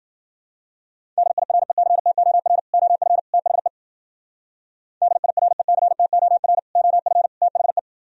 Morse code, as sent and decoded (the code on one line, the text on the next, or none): -.. .. .-. -.-. - --- .-. / --- ..-. / - .... / -.. .. .-. -.-. - --- .-. / --- ..-. / - .... .
DIRECTOR OF THE DIRECTOR OF THE